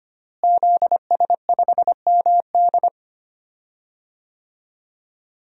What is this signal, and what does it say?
Morse code, heard as a keyed tone, about 25 wpm, 700 Hz. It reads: ZS5MD
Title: DXCC-100 Call Signs - 5 Characters - Repeated 2x - 25wpm